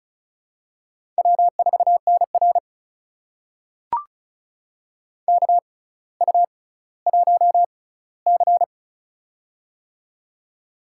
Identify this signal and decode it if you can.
Morse code, heard as a keyed tone, about 36 wpm, spaced as for 18 wpm, 700 Hz. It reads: W4NR KU1C
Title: USA Call Signs - 4 Characters - 6x Character Spacing - 35wpm